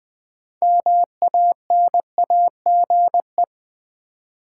Morse code, heard as a keyed tone, about 20 wpm, 700 Hz. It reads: MANAGE
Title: Build A Word - 2 Letters At A Time - 6-Letter Words - 20wpm